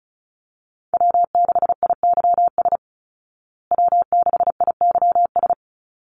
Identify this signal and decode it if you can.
Morse code, heard as a keyed tone, about 34 wpm, 700 Hz. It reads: W6IYS W6IYS
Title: Call Signs Repeated 3x - 35wpm